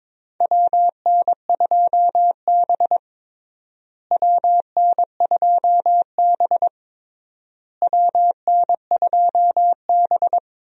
Morse code, heard as a keyed tone, about 22 wpm, 700 Hz. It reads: WN2B WN2B WN2B
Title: USA Call Signs - 4 Characters - Repeated 3x - 22wpm